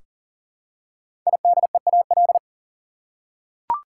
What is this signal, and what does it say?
Morse code, heard as a keyed tone, about 39 wpm, 700 Hz. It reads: IDEAL
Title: Five Letter Words - 40wpm